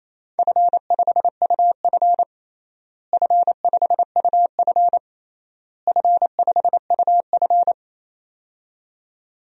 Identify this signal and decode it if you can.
Morse code, heard as a keyed tone, about 28 wpm, 700 Hz. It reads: F5UF F5UF F5UF
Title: DXCC-100 Call Signs - 4 Characters - Repeated 3x - 28wpm